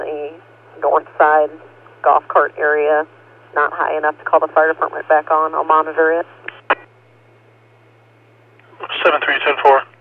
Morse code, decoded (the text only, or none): none